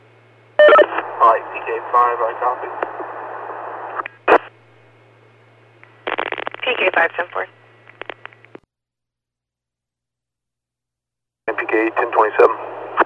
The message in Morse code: none

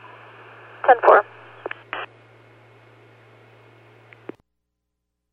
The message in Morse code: none

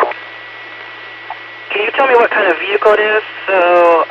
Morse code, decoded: none

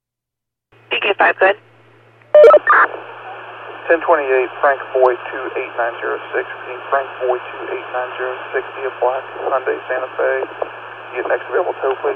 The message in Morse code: none

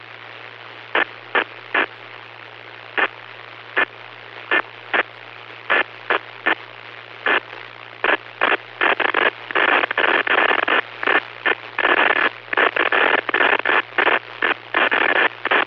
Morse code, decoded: none

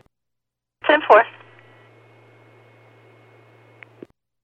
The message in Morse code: none

